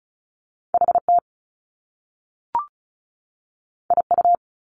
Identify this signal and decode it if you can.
Morse code, heard as a keyed tone, about 36 wpm, 700 Hz. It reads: HT IU
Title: Top Two Letter Combinations in Words 35wpm